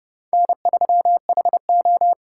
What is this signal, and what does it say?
Morse code, heard as a keyed tone, about 30 wpm, 700 Hz. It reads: N3HO
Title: USA Call Signs - 3x Character Spacing - 30wpm